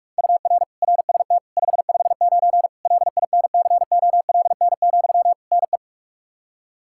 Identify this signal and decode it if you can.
Morse code, heard as a keyed tone, about 46 wpm, 700 Hz. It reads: UR RST 559 LINCOLN, NE
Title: Practice QSO - Exchange 1 - Beginning - 45wpm